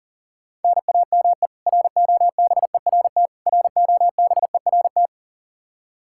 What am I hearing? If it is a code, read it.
Morse code, heard as a keyed tone, about 40 wpm, 700 Hz. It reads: NAME ROBERT ROBERT
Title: Practice QSO - Exchange 1 - Beginning - 40wpm